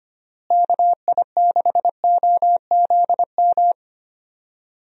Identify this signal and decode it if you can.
Morse code, heard as a keyed tone, about 25 wpm, 700 Hz. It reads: KI6OZM
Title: USA Call Signs - 6 Characters - Repeated 3x - 25wpm